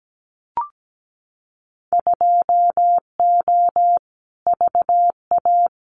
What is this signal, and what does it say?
Morse code, heard as a keyed tone, about 17 wpm, 700 Hz. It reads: 2O VA
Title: Winter Field Day Exchanges 17wpm